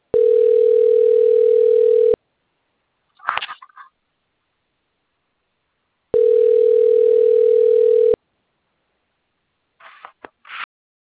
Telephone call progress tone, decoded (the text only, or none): ringback tone